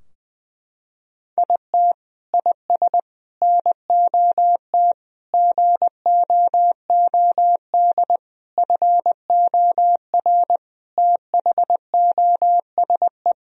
Morse code, heard as a keyed tone, about 21 wpm, 700 Hz. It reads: IT IS NOT GOOD FOR THOSE
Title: Sentences from Top 1000 Words - Review Part 2 - 20wpm